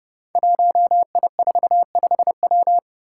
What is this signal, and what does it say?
Morse code, heard as a keyed tone, about 28 wpm, 700 Hz. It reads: 1I45W
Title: Five Challenging Letter-Number Combinations 30wpm